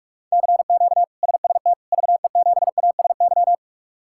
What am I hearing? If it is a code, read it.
Morse code, heard as a keyed tone, about 46 wpm, 700 Hz. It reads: CQ SST VE7ASY
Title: K1USN SST QSO Practice - 45wpm